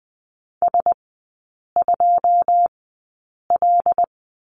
Morse code, heard as a keyed tone, about 20 wpm, spaced as for 8 wpm, 700 Hz. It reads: S2L
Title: 107 Lesson 18e - Letter m - Three Characters 20wpm